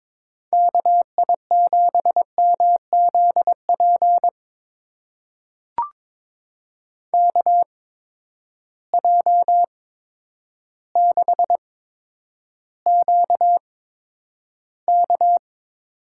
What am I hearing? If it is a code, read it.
Morse code, heard as a keyed tone, about 22 wpm, spaced as for 10 wpm, 700 Hz. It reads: KI7MZP KJ6QK
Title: USA Call Signs - 8x Character Spacing - 22wpm